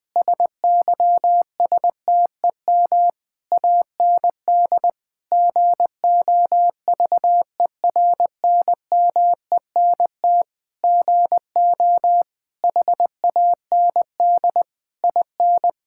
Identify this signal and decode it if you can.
Morse code, heard as a keyed tone, about 20 wpm, 700 Hz. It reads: SYSTEM AND GOVERNMENT GO HAND IN
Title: Sentences from Top 1000 Words - Review Part 5 - 20wpm